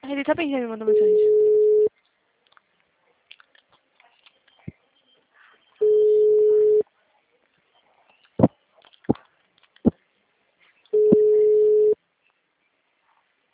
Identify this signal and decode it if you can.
Ringback tone (Europe, 425 Hz), with speech.
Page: Cordeiropolis